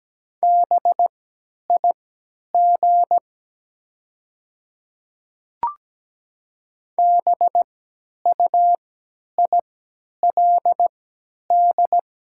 Morse code, heard as a keyed tone, about 17 wpm, spaced as for 10 wpm, 700 Hz. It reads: BIG BUILD
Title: 3x Character Spacing - Top 500 Words - 17wpm